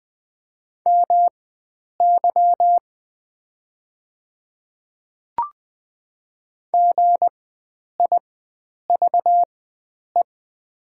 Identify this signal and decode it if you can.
Morse code, heard as a keyed tone, about 20 wpm, spaced as for 9 wpm, 700 Hz. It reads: MY GIVE
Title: 4x Character Spacing - Top 500 Words - 20wpm